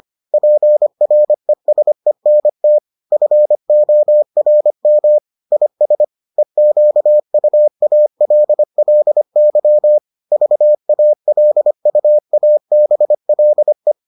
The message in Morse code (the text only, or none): PRESENT FORM IS EQUALLY VALUABLE